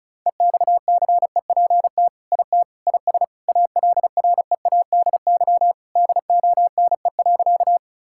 Morse code, heard as a keyed tone, about 36 wpm, 700 Hz. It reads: EXCEPT IT IS ALREADY DONE.
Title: Sentences from Top 1600 Words 35wpm